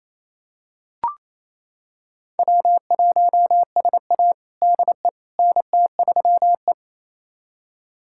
Morse code, heard as a keyed tone, about 28 wpm, 700 Hz. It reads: W1SA DE NT3E
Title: USA Callsign DE Callsign - 4 Characters - 28wpm